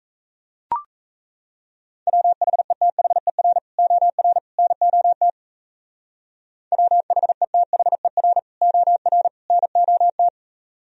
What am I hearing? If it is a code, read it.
Morse code, heard as a keyed tone, about 35 wpm, 700 Hz. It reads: WHETHER OR NOT WHETHER OR NOT
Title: Speed-Racing - Sets of 3 Words 28wpm